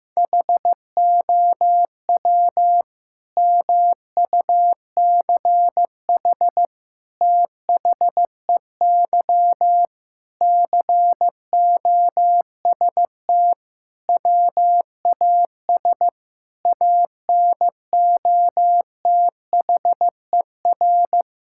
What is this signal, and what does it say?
Morse code, heard as a keyed tone, about 15 wpm, 700 Hz. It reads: HOW MUCH THEY COST WAS ANOTHER
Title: Sentences from Top 1000 Words - Review Part 5 - 15wpm